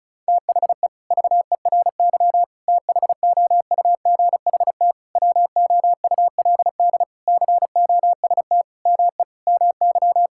Morse code, heard as a keyed tone, about 36 wpm, 700 Hz. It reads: THE VERY THOUGHT WOULD COST ME MY
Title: Sentences from Top 1000 Words - Review Part 5 - 35wpm